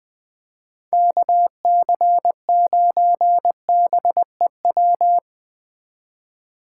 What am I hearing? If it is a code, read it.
Morse code, heard as a keyed tone, about 20 wpm, 700 Hz. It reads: KC9BEW
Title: Speed Racing - USA 2x3 Call Signs 17wpm